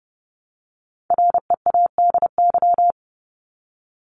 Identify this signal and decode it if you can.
Morse code, heard as a keyed tone, about 30 wpm, 700 Hz. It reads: READY
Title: Top 500 Words 30wpm